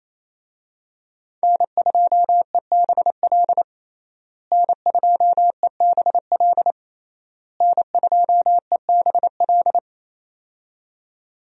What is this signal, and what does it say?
Morse code, heard as a keyed tone, about 28 wpm, 700 Hz. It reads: N2EBL N2EBL N2EBL
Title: USA Call Signs - 5 Characters - Repeated 3x - 28wpm